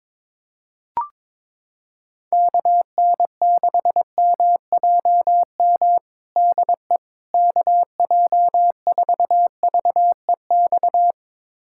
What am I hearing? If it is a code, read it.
Morse code, heard as a keyed tone, about 22 wpm, 700 Hz. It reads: KN6MJM DE KJ4VEX
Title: USA Callsign DE Callsign - 6 Characters - 22wpm